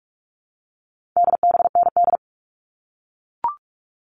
Morse code, heard as a keyed tone, about 44 wpm, 700 Hz. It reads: DBND
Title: All Cut Numbers - 4-Digit 45wpm